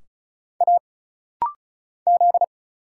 Morse code, heard as a keyed tone, about 36 wpm, 700 Hz. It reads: A Z
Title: Single Letter-Number - Rapid-Fire 35wpm